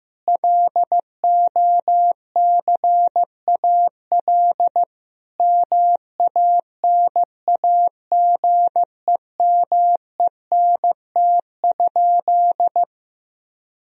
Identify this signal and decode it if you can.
Morse code, heard as a keyed tone, about 15 wpm, 700 Hz. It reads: LOCAL MANAGEMENT?
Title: Sentences from Top 1000 Words - Review Part 4 - 15wpm